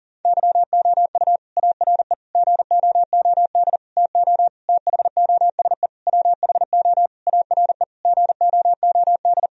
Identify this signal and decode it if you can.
Morse code, heard as a keyed tone, about 40 wpm, 700 Hz. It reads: YOU ARE GOOD TO THOSE WHO ARE GOOD
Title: Sentences from Top 1000 Words - Review Part 1 - 40wpm